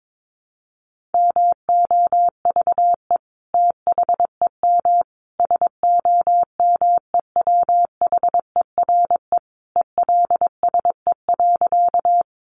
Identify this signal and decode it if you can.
Morse code, heard as a keyed tone, about 22 wpm, 700 Hz. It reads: MOVE THEM SOMEWHERE ELSE.
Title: Sentences from Top 1400 Words 22wpm